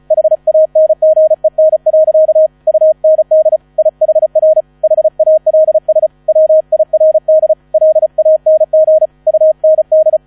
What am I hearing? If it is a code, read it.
Morse code, heard as a keyed tone, about 34 wpm, 630 Hz. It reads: HANGEN. UND IHR HALS WIRD LANG UND
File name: 35wpm